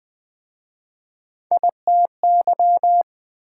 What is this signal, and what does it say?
Morse code, heard as a keyed tone, about 26 wpm, 700 Hz. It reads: ITY